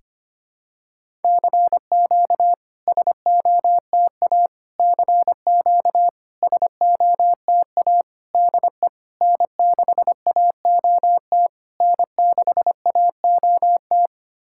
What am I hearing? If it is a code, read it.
Morse code, heard as a keyed tone, about 25 wpm, 700 Hz. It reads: CQ SOTA CQ SOTA DE N6AOT N6AOT
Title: SOTA QSO Practice 25wpm